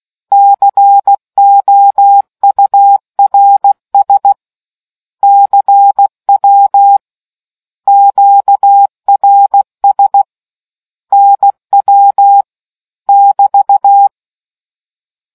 Morse code, heard as a keyed tone, about 16 wpm, 800 Hz. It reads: COURS CW QRS NW <BT>